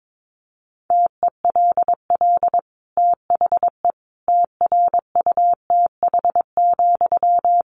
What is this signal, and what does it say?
Morse code, heard as a keyed tone, about 22 wpm, 700 Hz. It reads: TELL THE TRUTH,
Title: Sentences from Top 800 Words 22wpm